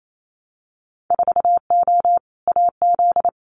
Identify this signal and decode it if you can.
Morse code, heard as a keyed tone, about 28 wpm, 700 Hz. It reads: 4O AZ
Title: Winter Field Day Exchanges Encore 28wpm